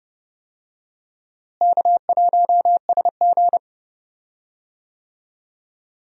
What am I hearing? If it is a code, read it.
Morse code, heard as a keyed tone, about 30 wpm, 700 Hz. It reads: K1SG
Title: USA Call Signs - 4 Characters - Repeated 2x - 30wpm